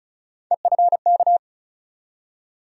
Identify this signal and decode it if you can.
Morse code, heard as a keyed tone, about 36 wpm, 700 Hz. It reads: EFK
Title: USA Call Sign Postfixes - 3 Letters 35wpm